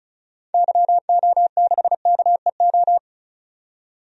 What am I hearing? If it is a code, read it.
Morse code, heard as a keyed tone, about 36 wpm, 700 Hz. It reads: YO6KEO
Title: DXCC-100 Call Signs - 6 Characters - 35wpm